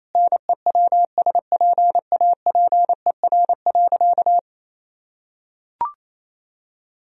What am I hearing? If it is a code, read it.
Morse code, heard as a keyed tone, about 27 wpm, 700 Hz. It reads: NEWSPAPER.
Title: Sentences from Top 1000 Words - Review Part 4 - 28wpm